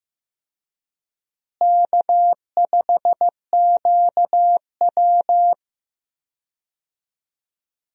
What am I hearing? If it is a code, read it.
Morse code, heard as a keyed tone, about 15 wpm, 700 Hz. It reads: K5QW
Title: USA Call Signs - 4 Characters - Repeated 3x - 15wpm